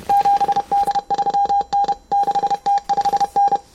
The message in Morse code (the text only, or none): 7D3N 6T5N